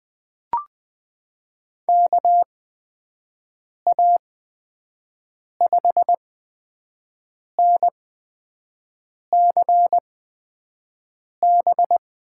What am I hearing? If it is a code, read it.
Morse code, heard as a keyed tone, about 20 wpm, spaced as for 5 wpm, 700 Hz. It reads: KA5NCB
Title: USA Call Signs - 6 Characters - 8x Character Spacing - 20wpm